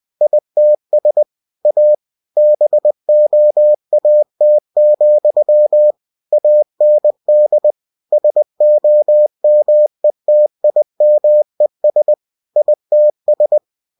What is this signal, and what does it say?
Morse code, heard as a keyed tone, about 20 wpm, 600 Hz. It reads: ITS A BOAT, AND SOMETIMES ITS
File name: cw-books / winnie-the-pooh / winnie-20wpm-page-0099